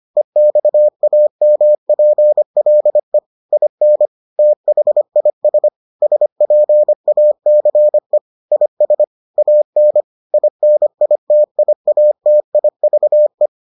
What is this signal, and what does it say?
Morse code, heard as a keyed tone, about 25 wpm, 600 Hz. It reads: EXAMPLE IN THIS SPACE IS AN INITIATIVE